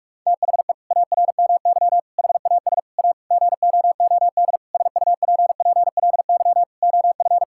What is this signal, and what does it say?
Morse code, heard as a keyed tone, about 46 wpm, 700 Hz. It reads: THE ARMY HAS A GOOD SUPPLY OF
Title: Sentences from Top 1000 Words - Review Part 5 - 45wpm